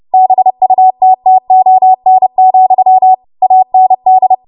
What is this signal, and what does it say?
Morse code, heard as a keyed tone, about 30 wpm, 750 Hz. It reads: BUTTON, AND
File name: Morse / 30WPM